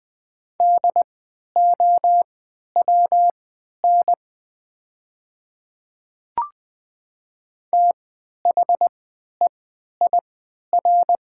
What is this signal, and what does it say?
Morse code, heard as a keyed tone, about 20 wpm, spaced as for 11 wpm, 700 Hz. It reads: DOWN THEIR
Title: 3x Character Spacing - Top 300 Words - 20wpm